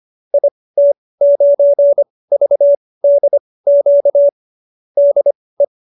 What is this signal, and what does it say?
Morse code, heard as a keyed tone, about 25 wpm, spaced as for 18 wpm, 570 Hz. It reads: IT9VDQ DE
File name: qso215b 18